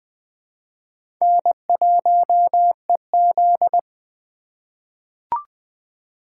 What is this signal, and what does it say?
Morse code, heard as a keyed tone, about 20 wpm, 700 Hz. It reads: N1EZ
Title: USA Call Signs - 4 Characters - 4x Character Spacing - 20wpm